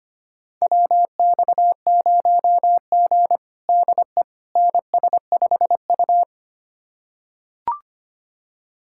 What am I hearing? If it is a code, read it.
Morse code, heard as a keyed tone, about 25 wpm, 700 Hz. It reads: WX0G DE NS5U
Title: USA Callsign DE Callsign - 4 Characters - 25wpm